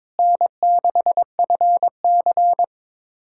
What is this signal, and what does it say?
Morse code, heard as a keyed tone, about 22 wpm, 700 Hz. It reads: N6FC
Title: USA Callsign DE Callsign - 4 Characters - 22wpm